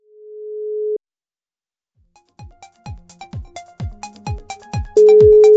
Ringback tone (Europe, 425 Hz).